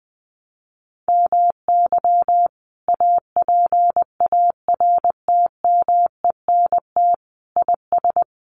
Morse code, heard as a keyed tone, about 20 wpm, 700 Hz. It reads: MY APARTMENT IS
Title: Sentences from Top 900 Words 20wpm